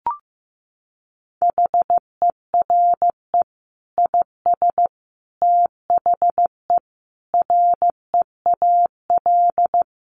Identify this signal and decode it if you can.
Morse code, heard as a keyed tone, about 15 wpm, 700 Hz. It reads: HERE IS THE REAL
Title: Sentences from Top 1500 Words 15wpm